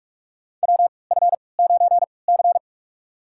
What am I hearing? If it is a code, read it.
Morse code, heard as a keyed tone, about 46 wpm, spaced as for 26 wpm, 700 Hz. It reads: WF9C
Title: USA Call Signs - 4 Characters - 3x Character Spacing - 45wpm